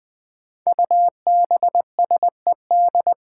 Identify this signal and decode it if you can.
Morse code, heard as a keyed tone, about 20 wpm, 700 Hz. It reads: UBSED
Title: Five Challenging Letter-Number Combinations 20wpm